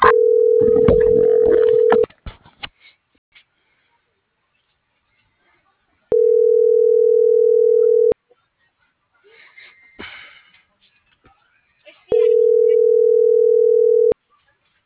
Ringback tone (North America), 440 + 480 Hz.